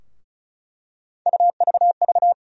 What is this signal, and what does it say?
Morse code, heard as a keyed tone, about 34 wpm, 700 Hz. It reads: UVV
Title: USA Call Sign Postfixes - 3 Letters 35wpm